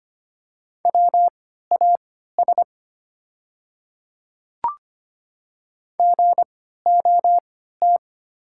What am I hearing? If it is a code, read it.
Morse code, heard as a keyed tone, about 25 wpm, spaced as for 14 wpm, 690 Hz. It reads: WAS GOT
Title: 3x Character Spacing - Top 500 Words - 25wpm